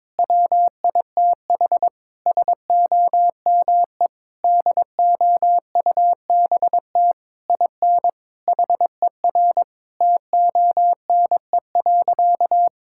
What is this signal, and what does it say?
Morse code, heard as a keyed tone, about 22 wpm, 700 Hz. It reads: WITH SOME DOUBT IN HER TONE.
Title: Sentences from Top 1600 Words 22wpm